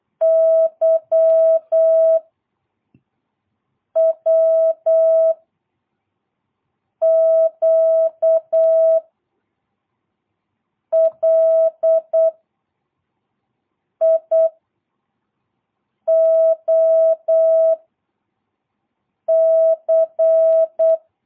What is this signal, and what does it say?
Morse code, heard as a keyed tone, about 8 wpm, 650 Hz. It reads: Y W Q L I O C
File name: jh8cbh / 2022cw